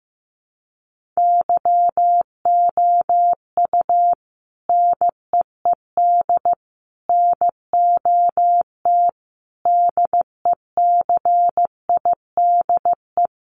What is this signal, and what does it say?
Morse code, heard as a keyed tone, about 15 wpm, 700 Hz. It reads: YOU NEED NOT DECIDE
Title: Sentences from Top 1500 Words 15wpm